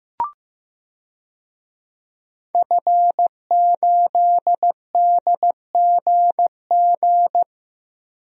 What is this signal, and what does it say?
Morse code, heard as a keyed tone, about 15 wpm, 700 Hz. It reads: F8DGG
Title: DXCC-100 Call Signs - 5 Characters - Repeated 3x - 15wpm